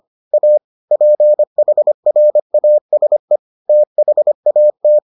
Morse code, heard as a keyed tone, about 25 wpm, 600 Hz. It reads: A PHRASE THAT